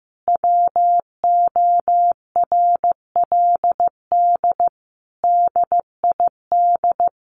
Morse code, heard as a keyed tone, about 15 wpm, 700 Hz. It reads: WORLD DID
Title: Sentences from Top 200 Words 15wpm